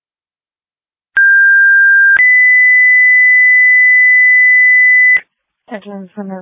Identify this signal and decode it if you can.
Two-tone page, heard as a one-second tone, then a three-second tone, about 1600 then 2000 Hz, with speech. It is heard from 1.2 to 5.2 s.